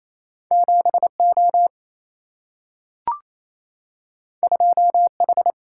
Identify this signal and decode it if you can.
Morse code, heard as a keyed tone, about 28 wpm, 700 Hz. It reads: 7O 2H
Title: Single-Letter Single-Number 28wpm